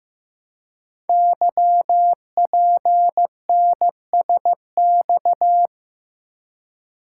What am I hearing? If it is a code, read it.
Morse code, heard as a keyed tone, about 15 wpm, 700 Hz. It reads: YPNSX
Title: Code Groups - 5 Letter:Numbers - 15wpm